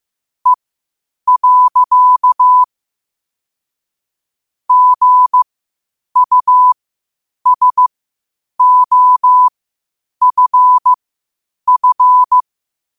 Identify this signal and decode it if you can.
Morse code, heard as a keyed tone, about 15 wpm, spaced as for 8 wpm, 1000 Hz. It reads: E. GUSOFF
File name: qotdfiles / 8wpm